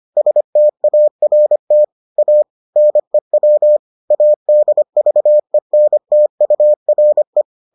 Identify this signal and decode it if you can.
Morse code, heard as a keyed tone, about 25 wpm, 600 Hz. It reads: START A NEW ADVENTURE